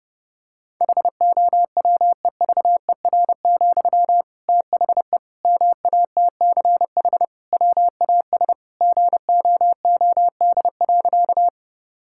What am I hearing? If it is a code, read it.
Morse code, heard as a keyed tone, about 30 wpm, 700 Hz. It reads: HOWEVER, THE MATCH WAS GOOD.
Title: Sentences from Top 1600 Words 30wpm